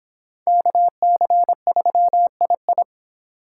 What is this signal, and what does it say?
Morse code, heard as a keyed tone, about 26 wpm, 700 Hz. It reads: KC3II